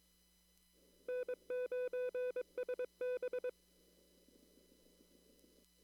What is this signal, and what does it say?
Morse code, heard as a keyed tone, about 23 wpm, 500 Hz. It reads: N9SB
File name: CW / kt5j wpx cw-op-k5pi